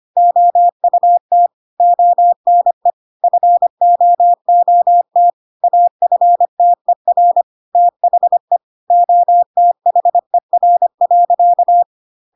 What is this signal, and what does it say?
Morse code, heard as a keyed tone, about 25 wpm, 700 Hz. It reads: OUT ONE FOOT AFTER THE OTHER.
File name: Pinocchio / Pinocchio ch3 25 25 ik1hge